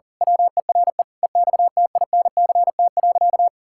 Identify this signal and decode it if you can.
Morse code, heard as a keyed tone, about 40 wpm, 700 Hz. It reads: WERE EXTINCT.
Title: Pangram Challenge v2 - 40wpm